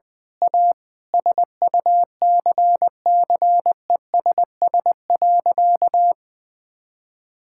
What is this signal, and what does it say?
Morse code, heard as a keyed tone, about 20 wpm, 700 Hz. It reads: A SUCCESS.
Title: Sentences with 9-Word Phrases - Spoken Beginning - 20wpm